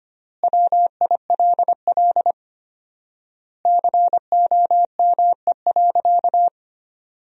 Morse code, heard as a keyed tone, about 25 wpm, 700 Hz. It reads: WILL COME.
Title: 4x Word-Spacing - Sentences from Top 100 Words - 25wpm